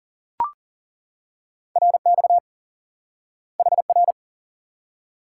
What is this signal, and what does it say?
Morse code, heard as a keyed tone, about 37 wpm, 700 Hz. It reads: RX HR
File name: IQE Course 3 - Lesson 89 - YAGI - Two Elements - 40wpm